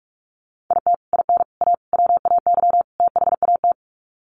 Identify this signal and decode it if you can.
Morse code, heard as a keyed tone, about 44 wpm, 700 Hz. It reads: IT IN A WAY THAT
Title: Sets of 5 Words Encore2 45wpm